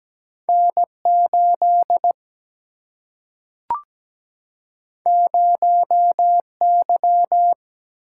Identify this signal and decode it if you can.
Morse code, heard as a keyed tone, about 17 wpm, 700 Hz. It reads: N8 0Y
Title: Single-Letter Single-Number 17wpm